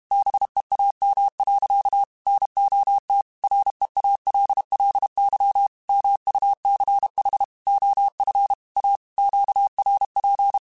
Morse code, heard as a keyed tone, about 32 wpm, 800 Hz. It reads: BEAM. NOT REALLY MUCH OF A QRP
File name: QSOs / QSOs-mp3s / WD0BGP 1@30